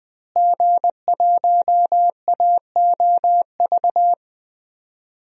Morse code, heard as a keyed tone, about 20 wpm, 700 Hz. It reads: G1AOV
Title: DXCC-100 Call Signs - 5 Characters - Repeated 3x - 20wpm